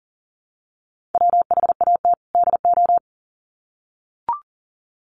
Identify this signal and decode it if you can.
Morse code, heard as a keyed tone, about 40 wpm, 700 Hz. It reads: WHAT DO
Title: Sets of 2 Words 40wpm